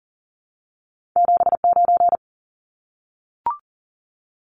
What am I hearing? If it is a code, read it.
Morse code, heard as a keyed tone, about 41 wpm, 700 Hz. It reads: 79
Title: Numbers - 2-Digit 40wpm